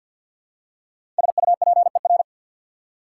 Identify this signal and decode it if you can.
Morse code, heard as a keyed tone, about 54 wpm, 690 Hz. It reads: SUPER
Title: Prefixes and Suffixes 50wpm